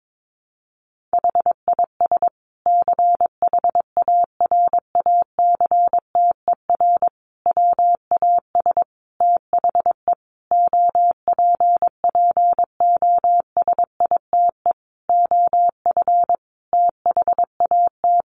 Morse code, heard as a keyed tone, about 22 wpm, 700 Hz. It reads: HIS CHARACTER WAS THE OPPOSITE OF THAT
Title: Sentences from Top 1300 Words 22wpm